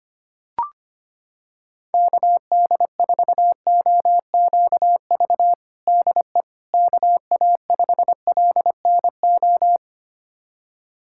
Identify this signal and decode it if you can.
Morse code, heard as a keyed tone, about 25 wpm, 700 Hz. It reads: KD4OQV DE KA5LNO
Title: USA Callsign DE Callsign - 6 Characters - 25wpm